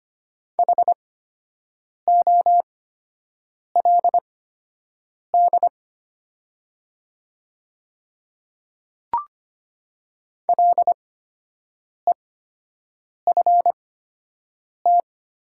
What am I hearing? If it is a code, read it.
Morse code, heard as a keyed tone, about 25 wpm, spaced as for 7 wpm, 700 Hz. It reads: HOLD LEFT
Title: 8x Character Spacing - Top 500 Words - 25wpm